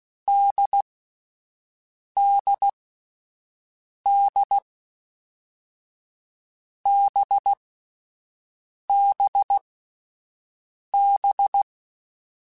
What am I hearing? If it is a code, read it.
Morse code, heard as a keyed tone, about 16 wpm, 800 Hz. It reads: D D D B B B
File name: Difficult Combos / Difficult Combos-mp3s / NDB NDB6 @10